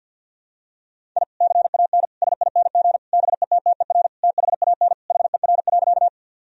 Morse code, heard as a keyed tone, about 49 wpm, 700 Hz. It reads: I CAN SING BETTER THAN HER.
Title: Sentences from Top 1000 Words - Review Part 2 - 50wpm